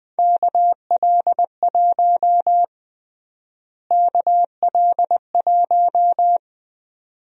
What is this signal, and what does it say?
Morse code, heard as a keyed tone, about 21 wpm, 700 Hz. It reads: KL1 KL1
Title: USA Call Sign Prefixes - Repeated 2x - 20wpm